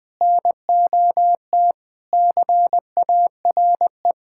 Morse code, heard as a keyed tone, about 20 wpm, 700 Hz. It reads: NOT CARE
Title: Sentences with 4-Word Phrases - Spoken Beginning - 20wpm